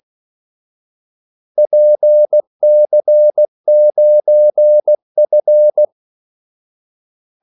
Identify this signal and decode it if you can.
Morse code, heard as a keyed tone, about 16 wpm, 600 Hz. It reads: PC9F
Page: Kaldesignaler 16 wpm